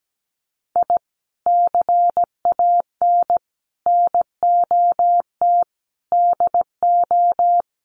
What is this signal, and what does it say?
Morse code, heard as a keyed tone, about 17 wpm, 700 Hz. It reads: I CAN NOT DO
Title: Sets of 5 Words Encore2 17wpm